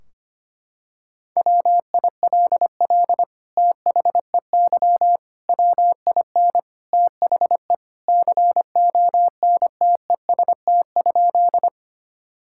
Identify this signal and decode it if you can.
Morse code, heard as a keyed tone, about 25 wpm, 700 Hz. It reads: WILL THEY WIN THE CONTEST?
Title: Sentences from Top 1600 Words 25wpm